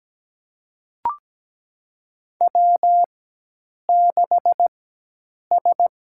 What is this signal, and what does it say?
Morse code, heard as a keyed tone, about 17 wpm, spaced as for 8 wpm, 700 Hz. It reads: W6S
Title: USA Call Signs - 4x Character Spacing - 17wpm